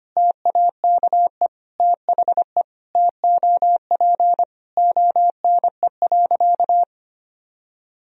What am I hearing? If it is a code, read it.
Morse code, heard as a keyed tone, about 25 wpm, 700 Hz. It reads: TAKE THE TOP ONE.
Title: Sentences from Top 1000 Words - Review Part 2 - 25wpm